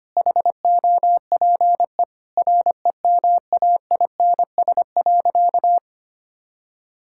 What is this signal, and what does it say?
Morse code, heard as a keyed tone, about 25 wpm, 700 Hz. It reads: HOPE REMAINS.
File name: practice / 25wpm / Sentences with 2-Word Phrases - 25wpm